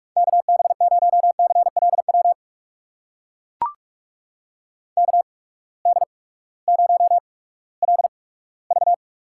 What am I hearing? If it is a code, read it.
Morse code, heard as a keyed tone, about 43 wpm, spaced as for 20 wpm, 700 Hz. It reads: KB0CLW KD0LV
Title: USA Call Signs - 6 Characters - 8x Character Spacing - 45wpm